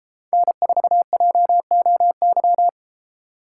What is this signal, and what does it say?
Morse code, heard as a keyed tone, about 33 wpm, 700 Hz. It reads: N4JOY